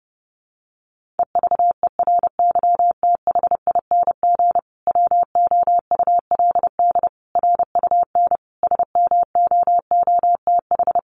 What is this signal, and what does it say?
Morse code, heard as a keyed tone, about 30 wpm, 700 Hz. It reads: EVERYTHING WOULD RUN SMOOTH
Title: Sentences from Top 1400 Words 30wpm